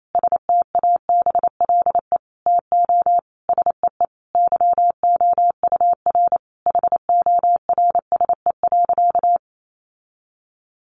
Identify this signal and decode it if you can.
Morse code, heard as a keyed tone, about 28 wpm, 700 Hz. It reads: STABLE TO SEE YOUR HORSE.
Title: Sentences from Top 1500 Words 28wpm